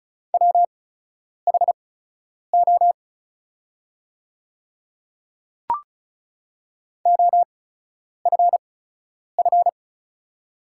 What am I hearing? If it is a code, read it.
Morse code, heard as a keyed tone, about 36 wpm, spaced as for 10 wpm, 690 Hz. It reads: WHO OFF
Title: 8x Character Spacing - Top 200 Words - 35wpm